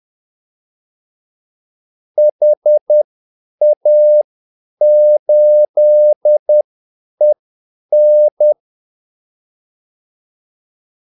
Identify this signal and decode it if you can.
Morse code, heard as a keyed tone, about 10 wpm, spaced as for 8 wpm, 600 Hz. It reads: HA8EN
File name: Callsigns-10-8wpm